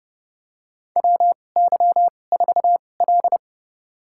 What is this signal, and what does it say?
Morse code, heard as a keyed tone, about 30 wpm, spaced as for 22 wpm, 700 Hz. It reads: WY4L
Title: USA Call Signs - 2x Character Spacing - 30wpm